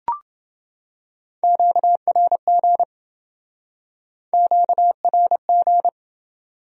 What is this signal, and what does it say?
Morse code, heard as a keyed tone, about 27 wpm, 700 Hz. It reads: QRG QRG